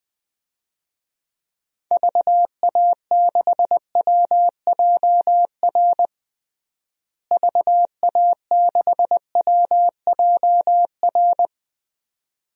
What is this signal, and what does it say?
Morse code, heard as a keyed tone, about 20 wpm, 700 Hz. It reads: VA6WJR VA6WJR
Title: DXCC-100 Call Signs - 6 Characters - Repeated 2x - 20wpm